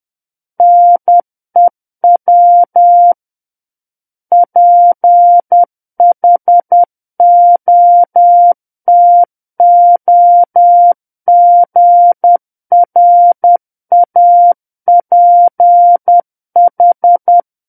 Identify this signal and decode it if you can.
Morse code, heard as a keyed tone, about 10 wpm, 700 Hz. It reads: NEW PHOTOGRAPH
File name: Book One / 01-Chapter 1 - 10 WPM